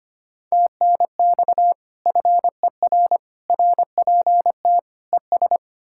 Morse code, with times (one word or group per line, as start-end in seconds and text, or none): 0.5-1.7 TNX
2.1-3.2 FER
3.5-4.8 RPT
5.1-5.6 ES